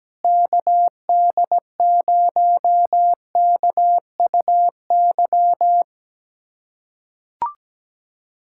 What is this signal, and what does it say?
Morse code, heard as a keyed tone, about 17 wpm, 700 Hz. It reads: KD0KUY
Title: USA Callsign DE Callsign 17wpm